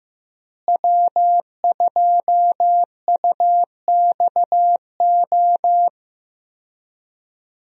Morse code, heard as a keyed tone, about 15 wpm, 700 Hz. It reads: W2UXO
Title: USA Call Signs - 5 Characters - Repeated 2x - 15wpm